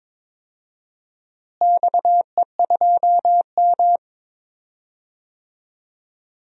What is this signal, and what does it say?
Morse code, heard as a keyed tone, about 22 wpm, 700 Hz. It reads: XE2M
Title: DXCC-100 Call Signs - 4 Characters - Repeated 3x - 22wpm